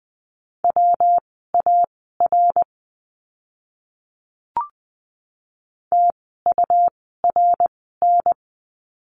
Morse code, heard as a keyed tone, about 20 wpm, spaced as for 15 wpm, 700 Hz. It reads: WAR TURN
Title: 115 Lesson 19f - Letter w - Words 20wpm